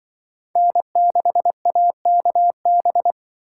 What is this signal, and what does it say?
Morse code, heard as a keyed tone, about 24 wpm, 700 Hz. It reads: N6AKB
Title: Speed Racing - USA 1x3 Call Signs 20wpm